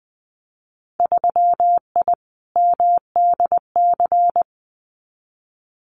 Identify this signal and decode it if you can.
Morse code, heard as a keyed tone, about 19 wpm, 700 Hz. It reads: 3I MDC
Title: Winter Field Day Exchanges Encore 20wpm